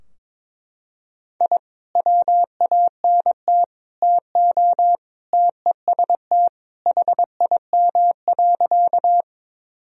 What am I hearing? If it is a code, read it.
Morse code, heard as a keyed tone, about 22 wpm, 700 Hz. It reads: I WANT TO TEST HIM.
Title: Sentences from Top 1000 Words - Review Part 3 - 22wpm